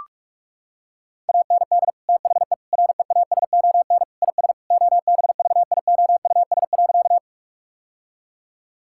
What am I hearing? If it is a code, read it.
Morse code, heard as a keyed tone, about 46 wpm, 700 Hz. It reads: AND THE REASON IS OBVIOUS.
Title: Sentences from Top 1600 Words 45wpm